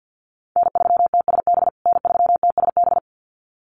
Morse code, heard as a keyed tone, about 50 wpm, 700 Hz. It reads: N3TSB N3TSB
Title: USA Call Signs Repeated 2x 50wpm